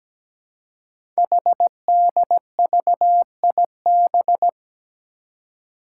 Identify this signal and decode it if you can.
Morse code, heard as a keyed tone, about 17 wpm, 700 Hz. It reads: HDVIB
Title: Code Groups - 5 Letters - 17wpm